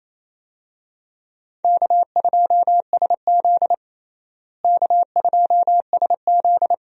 Morse code, heard as a keyed tone, about 28 wpm, 700 Hz. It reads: K2SZ K2SZ
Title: USA Call Signs - 4 Characters - Repeated 2x - 28wpm